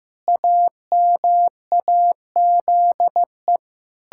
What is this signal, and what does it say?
Morse code, heard as a keyed tone, about 15 wpm, 700 Hz. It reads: AMAZE
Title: Pangram Challenge v2 - 15wpm